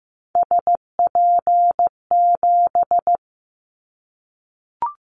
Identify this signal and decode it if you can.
Morse code, heard as a keyed tone, about 15 wpm, 700 Hz. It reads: SP7
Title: Callsign Prefixes for Top 25 DXCC - 15wpm